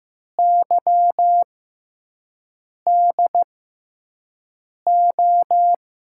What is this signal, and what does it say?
Morse code, heard as a keyed tone, about 15 wpm, spaced as for 5 wpm, 700 Hz. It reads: YDO
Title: USA Call Signs - 6x Character Spacing - 15wpm